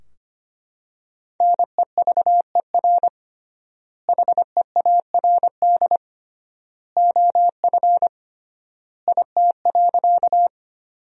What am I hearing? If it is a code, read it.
Morse code, heard as a keyed tone, about 25 wpm, 700 Hz. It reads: NEVER HEARD OF IT.
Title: 3x Word-Spacing - Sentences from Top 400 Words - 25wpm